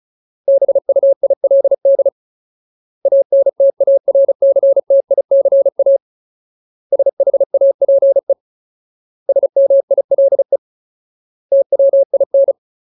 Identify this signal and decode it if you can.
Morse code, heard as a keyed tone, about 34 wpm, 550 Hz. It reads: BUILD ANTARCTICA SHAPE SMILE TWIN
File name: Ultimate Word List / 35WPM Ultimate